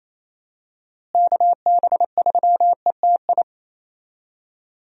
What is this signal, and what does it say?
Morse code, heard as a keyed tone, about 28 wpm, 700 Hz. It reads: KB3ETI
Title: USA Call Signs - 6 Characters - Repeated 3x - 28wpm